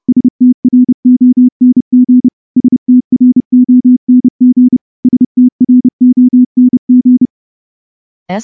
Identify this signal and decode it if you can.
Morse code, heard as a keyed tone, about 28 wpm, 250 Hz. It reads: STRONG STRONG STRONG
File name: morse-training-30WPM